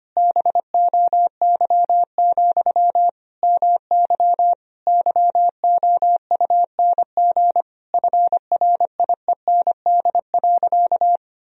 Morse code, heard as a keyed tone, about 25 wpm, 700 Hz. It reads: BOY, MY YOUNG FRIEND.
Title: Sentences from Top 1000 Words - Review Part 3 - 25wpm